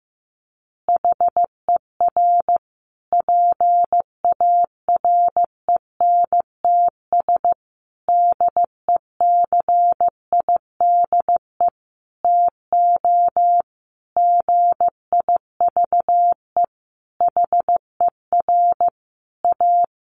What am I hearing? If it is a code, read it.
Morse code, heard as a keyed tone, about 15 wpm, 700 Hz. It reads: HER PARENTS DECIDE TO GIVE HER A
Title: Easy Phrases 15wpm